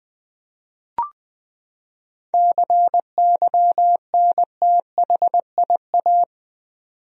Morse code, heard as a keyed tone, about 20 wpm, 700 Hz. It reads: CYNTHIA
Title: USA Common Names - 20wpm